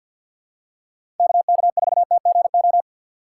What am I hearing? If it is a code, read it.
Morse code, heard as a keyed tone, about 53 wpm, 700 Hz. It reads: KK4TGO
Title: USA Call Signs - 4x Character Spacing - 50wpm